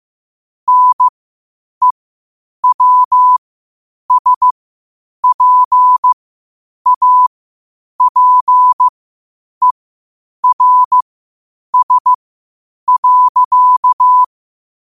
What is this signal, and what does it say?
Morse code, heard as a keyed tone, about 15 wpm, spaced as for 8 wpm, 1000 Hz. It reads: NEWSPAPERS.
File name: qotdfiles / 8wpm